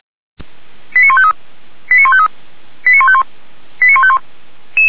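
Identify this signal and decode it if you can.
Five-tone selective call (ZVEI), heard as quick runs of five tones, tones of about 70 ms. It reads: 89153 89153 89151 89151
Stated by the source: Received on an SDR.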